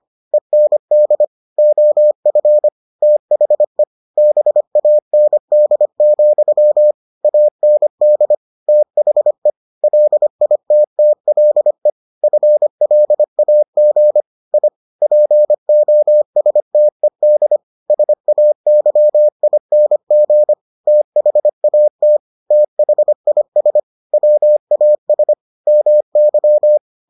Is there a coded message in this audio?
Yes: Morse code from 0.3 to 26.8 s, heard as a keyed tone, about 25 wpm, 600 Hz. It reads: END OF THE BAND, AND THE LITTLE FLAG I POSTED SAYING THAT THIS WAS MY